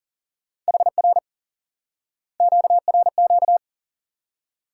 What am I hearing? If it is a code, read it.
Morse code, heard as a keyed tone, about 40 wpm, 700 Hz. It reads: HR QRQ
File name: IQE Course 3 - Lesson 89 - YAGI - Two Elements - 40wpm